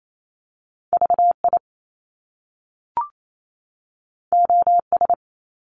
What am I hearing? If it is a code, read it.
Morse code, heard as a keyed tone, about 28 wpm, 700 Hz. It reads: VI OS
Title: Top Two Letter Combinations in Words 28wpm